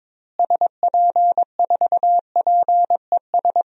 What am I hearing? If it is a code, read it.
Morse code, heard as a keyed tone, about 21 wpm, 700 Hz. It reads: SP4PES
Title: DXCC-100 Call Signs - 6 Characters - Repeated 2x - 22wpm